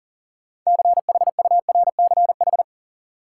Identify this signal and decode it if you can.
Morse code, heard as a keyed tone, about 40 wpm, 700 Hz. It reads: CHURCH
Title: Sets of 5 Words - New Word Drill - 40wpm